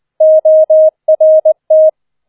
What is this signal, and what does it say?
Morse code, heard as a keyed tone, about 20 wpm, 630 Hz. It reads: ORT